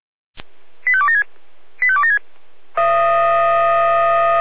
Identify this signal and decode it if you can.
Five-tone selective call (ZVEI), heard as quick runs of five tones, tones of about 70 ms. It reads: 85276 85276
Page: SDR/ZVEI